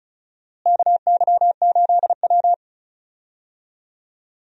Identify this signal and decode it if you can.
Morse code, heard as a keyed tone, about 34 wpm, 700 Hz. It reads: KY8W
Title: USA Call Signs - 4 Characters - Repeated 3x - 35wpm